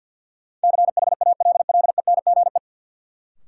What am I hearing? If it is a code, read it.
Morse code, heard as a keyed tone, about 48 wpm, 700 Hz. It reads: CHALLENGE
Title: Sentences from Top 1000 Words - Review Part 7 - New Word Drill - 50wpm